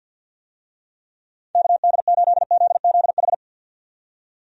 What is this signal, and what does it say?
Morse code, heard as a keyed tone, about 50 wpm, 700 Hz. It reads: KD8ZZH
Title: USA Call Signs - 6 Characters - 50wpm